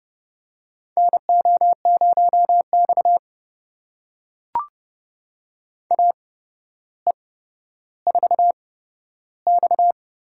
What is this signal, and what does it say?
Morse code, heard as a keyed tone, about 30 wpm, spaced as for 13 wpm, 700 Hz. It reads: NO0X AE4X
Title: USA Call Signs - 4 Characters - 8x Character Spacing - 30wpm